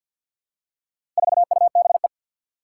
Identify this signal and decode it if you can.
Morse code, heard as a keyed tone, about 49 wpm, 700 Hz. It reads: 4UBE
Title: Four Challenging Letter-Number Combinations 50wpm